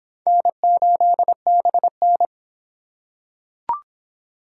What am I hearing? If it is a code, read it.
Morse code, heard as a keyed tone, about 25 wpm, 700 Hz. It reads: N8BN